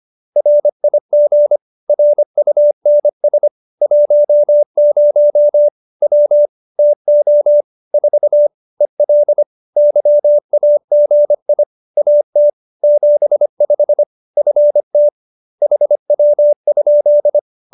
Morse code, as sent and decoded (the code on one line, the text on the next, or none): .-. .. --. / .-. ..- -. ... / .---- ----- / .-- / - --- / ....- / . .-.. / -.-- .- --. .. / .- - / --... ..... / ..-. - / .... .-- ..--..
RIG RUNS 10 W TO 4 EL YAGI AT 75 FT HW?